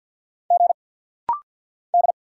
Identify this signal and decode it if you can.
Morse code, heard as a keyed tone, about 50 wpm, 690 Hz. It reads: G D
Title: Single Letters - Rapid-Fire 50wpm